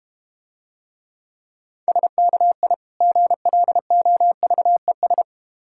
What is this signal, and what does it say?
Morse code, heard as a keyed tone, about 32 wpm, 700 Hz. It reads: SKI GLOVES